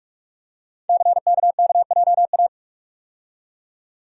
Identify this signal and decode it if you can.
Morse code, heard as a keyed tone, about 44 wpm, 700 Hz. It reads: CKKJA
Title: Code Groups - 5 Letters - 45wpm